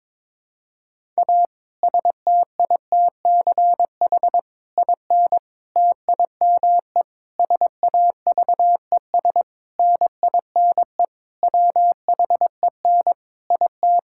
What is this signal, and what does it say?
Morse code, heard as a keyed tone, about 22 wpm, 700 Hz. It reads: A STITCH IN TIME SAVES NINE WHEN IT
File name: Difficult Sentences with Hints - 22wpm